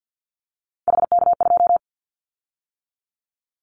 Morse code, heard as a keyed tone, about 50 wpm, 700 Hz. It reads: HK2
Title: Callsign Prefixes for Top 100 DXCC - 50wpm